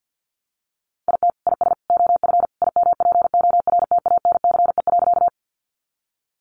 Morse code, heard as a keyed tone, about 50 wpm, 700 Hz. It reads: IT IS OF IMPORTANCE.
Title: Sentences from Top 1200 Words 50wpm